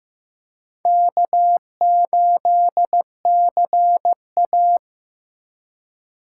Morse code, heard as a keyed tone, about 18 wpm, 700 Hz. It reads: K8CA
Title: Speed Racing - USA 1x2 Call Signs 15wpm